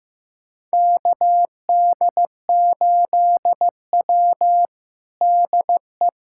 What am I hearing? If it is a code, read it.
Morse code, heard as a keyed tone, about 15 wpm, 700 Hz. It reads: KD8W DE
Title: USA Callsign DE Callsign - 4 Characters - 15wpm